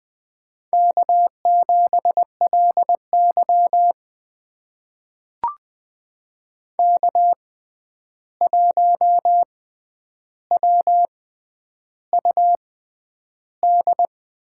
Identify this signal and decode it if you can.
Morse code, heard as a keyed tone, about 20 wpm, spaced as for 10 wpm, 700 Hz. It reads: K7LY K1WUD
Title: USA Call Signs - 6x Character Spacing - 20wpm